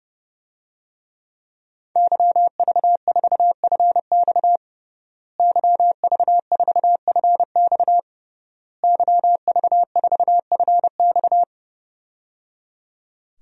Morse code, heard as a keyed tone, about 30 wpm, 700 Hz. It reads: YV4FX YV4FX YV4FX
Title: DXCC-100 Call Signs - 5 Characters - Repeated 3x - 30wpm